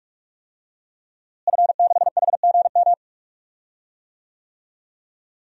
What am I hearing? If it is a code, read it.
Morse code, heard as a keyed tone, about 44 wpm, 700 Hz. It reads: F6HGM
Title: DXCC-100 Call Signs - 5 Characters - Repeated 2x - 45wpm